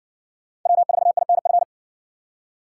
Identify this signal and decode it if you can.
Morse code, heard as a keyed tone, about 54 wpm, 700 Hz. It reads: W4INF